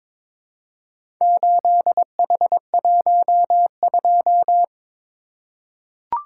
Morse code, heard as a keyed tone, about 22 wpm, 700 Hz. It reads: 8H12
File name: practice / 22wpm / Four Letter-Number Combinations - 22wpm